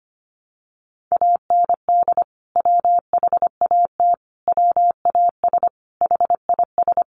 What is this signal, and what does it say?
Morse code, heard as a keyed tone, about 25 wpm, 700 Hz. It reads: AND WHAT WAS HIS
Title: Sentences from Top 200 Words 25wpm